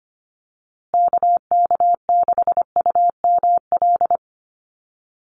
Morse code, heard as a keyed tone, about 25 wpm, 700 Hz. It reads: KK6UML
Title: USA Call Signs Repeated 3x 25wpm